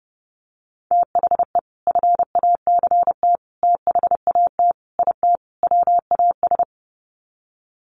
Morse code, heard as a keyed tone, about 30 wpm, 700 Hz. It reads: THE FACT THAT IT WAS
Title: Sets of 5 Words Encore2 30wpm